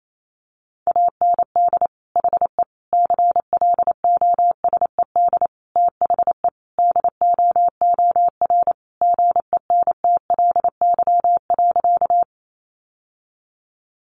Morse code, heard as a keyed tone, about 28 wpm, 700 Hz. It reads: AND HE CLOSED THE DOOR GENTLY.
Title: Sentences from Top 1500 Words 28wpm